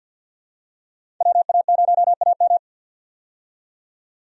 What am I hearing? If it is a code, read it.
Morse code, heard as a keyed tone, about 50 wpm, 700 Hz. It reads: WA0AM
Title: USA Call Signs - 5 Characters - 50wpm